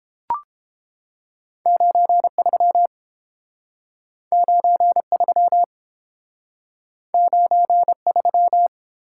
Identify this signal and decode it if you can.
Morse code, heard as a keyed tone, about 30 wpm, 700 Hz. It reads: 93 93 93